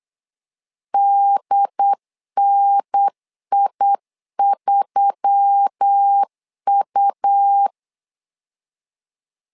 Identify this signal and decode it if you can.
Morse code, heard as a keyed tone, about 8 wpm, 800 Hz. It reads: DNI3U
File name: LEZIONI / 5 LBVC / 5 LB 8wpm 2816